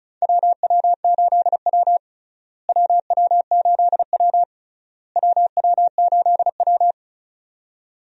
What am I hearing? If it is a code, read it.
Morse code, heard as a keyed tone, about 34 wpm, 700 Hz. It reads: WW8W WW8W WW8W
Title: USA Call Signs - 4 Characters - Repeated 3x - 35wpm